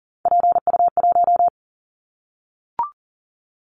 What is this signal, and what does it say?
Morse code, heard as a keyed tone, about 40 wpm, 700 Hz. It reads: PU1
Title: Callsign Prefixes for Top 100 DXCC - 40wpm